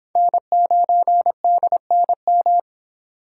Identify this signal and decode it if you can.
Morse code, heard as a keyed tone, about 29 wpm, 700 Hz. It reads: N9DNM